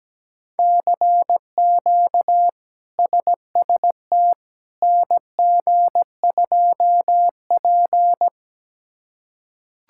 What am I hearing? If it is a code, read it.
Morse code, heard as a keyed tone, about 17 wpm, 700 Hz. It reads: CQ SST NG2P
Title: K1USN SST QSO Practice - 17wpm